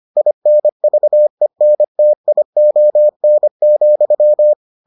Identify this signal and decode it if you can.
Morse code, heard as a keyed tone, about 25 wpm, 600 Hz. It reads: INVENTION,